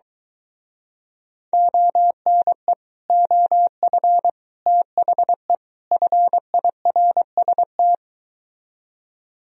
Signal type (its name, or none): Morse code